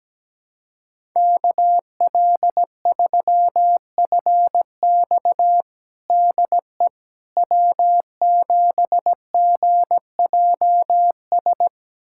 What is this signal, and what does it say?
Morse code, heard as a keyed tone, about 17 wpm, 700 Hz. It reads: KL3FX DE W7GJS
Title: USA Callsign DE Callsign - 5 Characters - 17wpm